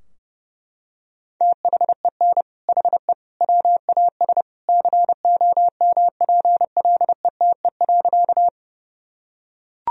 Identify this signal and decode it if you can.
Morse code, heard as a keyed tone, about 32 wpm, 700 Hz. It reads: THEN HE WAS COMPLETE.
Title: Sentences from Top 1000 Words - Review Part 2 - 30wpm